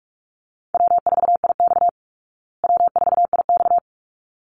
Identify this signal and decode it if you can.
Morse code, heard as a keyed tone, about 44 wpm, 700 Hz. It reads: W4IX W4IX
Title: Easy Call Signs Repeated 2x - 45wpm